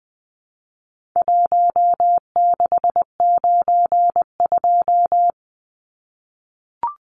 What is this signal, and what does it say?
Morse code, heard as a keyed tone, about 20 wpm, 700 Hz. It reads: 1692
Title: Numbers - 4-Digit 20wpm